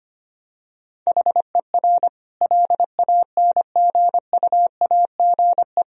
Morse code, heard as a keyed tone, about 25 wpm, 700 Hz. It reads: HER LANGUAGE
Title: Sentences from Top 1000 Words - Review Part 3 - 25wpm